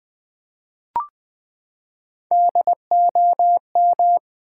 Morse code, heard as a keyed tone, about 20 wpm, 700 Hz. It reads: DOM
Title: Build A Word - 3 Letters At A Time - 8-Letter Words - 20wpm